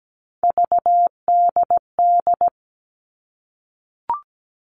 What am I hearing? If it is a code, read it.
Morse code, heard as a keyed tone, about 17 wpm, 700 Hz. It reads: VDD
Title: All Cut Numbers - 3-Digit 17wpm